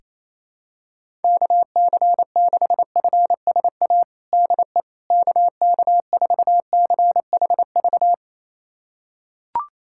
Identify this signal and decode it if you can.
Morse code, heard as a keyed tone, about 28 wpm, 700 Hz. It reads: KC6FSA DE KK4CHV
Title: USA Callsign DE Callsign 28wpm